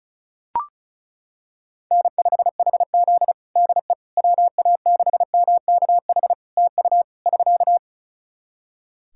Morse code, heard as a keyed tone, about 36 wpm, 700 Hz. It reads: N5HZ DE WA6MKH TU <SK>
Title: Practice QSO - Exchange 4 - Ending - 35wpm